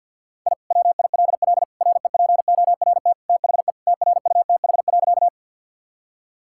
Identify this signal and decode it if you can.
Morse code, heard as a keyed tone, about 50 wpm, 700 Hz. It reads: I WILL REPORT THE TRUTH.
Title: Sentences from Top 1000 Words - Review Part 6 - 50wpm